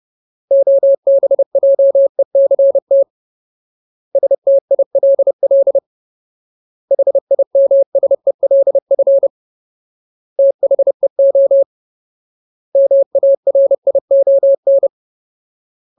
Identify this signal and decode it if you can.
Morse code, heard as a keyed tone, about 30 wpm, 550 Hz. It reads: OBJECT STILL HIMSELF THEO MARION
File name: Ultimate Word List / 30WPM Ultimate